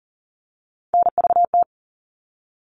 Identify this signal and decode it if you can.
Morse code, heard as a keyed tone, about 40 wpm, 700 Hz. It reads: NVT
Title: All Cut Numbers - 3-Digit 40wpm